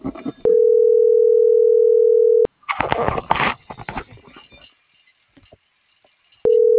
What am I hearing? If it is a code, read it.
Ringback tone (North America), 440 + 480 Hz.